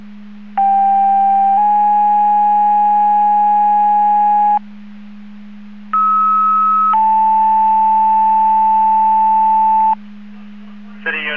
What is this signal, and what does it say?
Two-tone page, heard as a one-second tone, then a three-second tone, about 790 then 830 Hz, about 1290 then 880 Hz. It is heard from 0.6 to 9.9 s.
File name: Portage 154.385 Paging